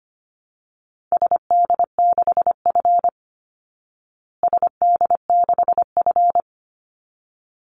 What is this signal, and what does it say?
Morse code, heard as a keyed tone, about 25 wpm, 700 Hz. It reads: SD6F SD6F
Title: Call Signs Repeated 3x - 25wpm